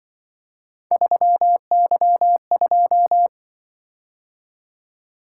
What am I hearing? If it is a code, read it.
Morse code, heard as a keyed tone, about 24 wpm, 700 Hz. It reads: 3Y2
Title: Speed-Racing - Three Letter-Number Combinations - 20wpm